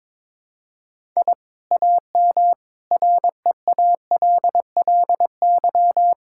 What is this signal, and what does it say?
Morse code, heard as a keyed tone, about 22 wpm, 700 Hz. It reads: I AM REALLY
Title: Sentences from Top 1000 Words - Review Part 6 - 22wpm